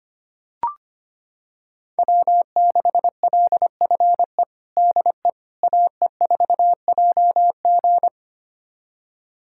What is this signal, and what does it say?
Morse code, heard as a keyed tone, about 25 wpm, 700 Hz. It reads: W6LFE DE AE4JG
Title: USA Callsign DE Callsign - 5 Characters - 25wpm